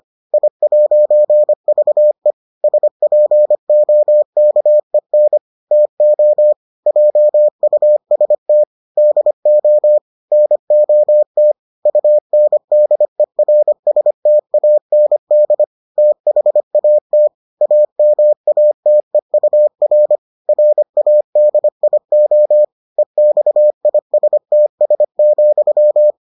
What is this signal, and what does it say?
Morse code, heard as a keyed tone, about 25 wpm, 600 Hz. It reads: I'VE SPOKEN TO JUST DO NOT UNDERSTAND THAT AMATEUR RADIO EXISTS,